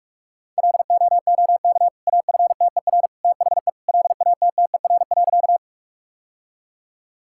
Morse code, heard as a keyed tone, about 46 wpm, 700 Hz. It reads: LOOK AFTER THE LATTER.
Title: Sentences from Top 1600 Words 45wpm